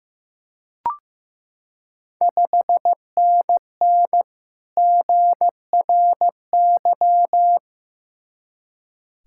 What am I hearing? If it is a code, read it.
Morse code, heard as a keyed tone, about 15 wpm, 700 Hz. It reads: 5NN GRY
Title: Ontario QSO Party Exchange 15wpm